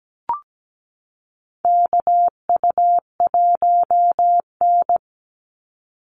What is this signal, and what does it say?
Morse code, heard as a keyed tone, about 17 wpm, 700 Hz. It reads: KU1N
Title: Call Signs Repeated 2x - 17wpm